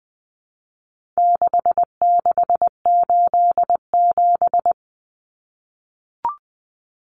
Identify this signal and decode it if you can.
Morse code, heard as a keyed tone, about 20 wpm, 700 Hz. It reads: 6687
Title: Common Cut Numbers - 4-Digit 20wpm